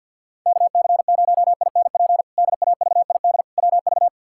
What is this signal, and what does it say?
Morse code, heard as a keyed tone, about 47 wpm, 700 Hz. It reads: KC0INP DAVID WV
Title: K1USN SST QSO Practice - 50wpm